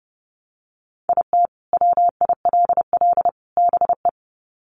Morse code, heard as a keyed tone, about 30 wpm, 700 Hz. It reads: IT WILL BE
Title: Sets of 3 Words 30wpm